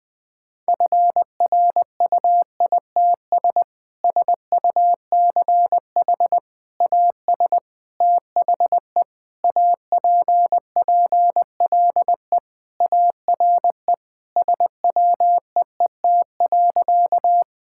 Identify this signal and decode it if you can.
Morse code, heard as a keyed tone, about 20 wpm, 700 Hz. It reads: FRUITS SUCH AS THE APPLE ARE SWEET.
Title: Sentences with 3-Word Phrases - 20wpm